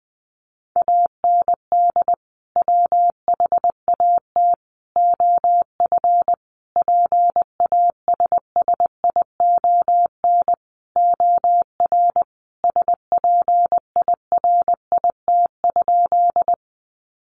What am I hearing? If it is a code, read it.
Morse code, heard as a keyed tone, about 20 wpm, 700 Hz. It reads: AND WHAT OF PASSION OR SPIRIT?
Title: Sentences from Top 1400 Words 20wpm